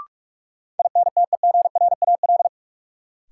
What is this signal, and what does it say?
Morse code, heard as a keyed tone, about 48 wpm, 700 Hz. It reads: INTEGRAL
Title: Sets of 5 Words - New Word Drill - 45wpm